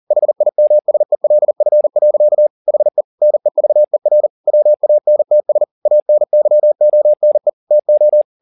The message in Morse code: HIMSELF. HE NEVER WANTS ANYONE TO

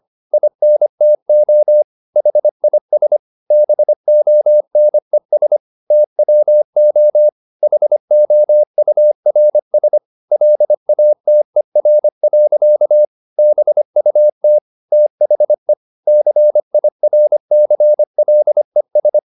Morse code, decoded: INTO HIS BONES TWO HOURS LATER. BUT THE CIRCLES

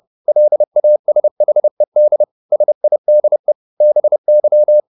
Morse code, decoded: LASHED SIDE BY

